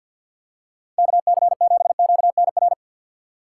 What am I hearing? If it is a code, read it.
Morse code, heard as a keyed tone, about 52 wpm, 700 Hz. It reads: KC7QNR
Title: USA Call Signs - 6 Characters - 4x Character Spacing - 50wpm